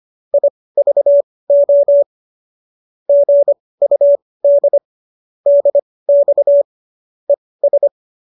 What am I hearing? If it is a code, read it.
Morse code, heard as a keyed tone, about 25 wpm, spaced as for 18 wpm, 570 Hz. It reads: IVO GUD DX ES